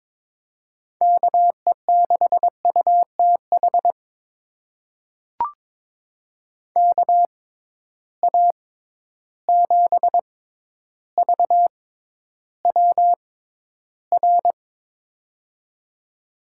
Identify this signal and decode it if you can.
Morse code, heard as a keyed tone, about 22 wpm, spaced as for 11 wpm, 700 Hz. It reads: KE6UTH KA7VWR
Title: USA Call Signs - 6 Characters - 6x Character Spacing - 22wpm